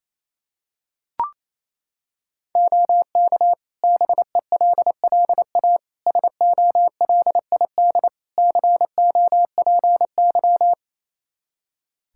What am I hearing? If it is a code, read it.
Morse code, heard as a keyed tone, about 27 wpm, 700 Hz. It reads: OK BELLA SOLID COPY
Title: K1USN SST QSO Practice 28wpm